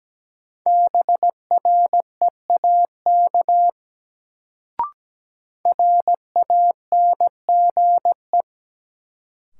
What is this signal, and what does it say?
Morse code, heard as a keyed tone, about 17 wpm, 700 Hz. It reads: BREAK RANGE
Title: Sentences from Top 900 Words - New Word Drill - 17wpm